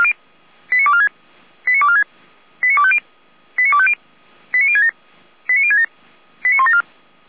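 Five-tone selective call (ZVEI), heard as quick runs of five tones, tones of about 70 ms. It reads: 89246 89246 89250 89250 89067 89067 89174
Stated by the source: ZVEI five-tone selective call 89108 received on an SDR, Feuerwehr Orlamünde fire brigade.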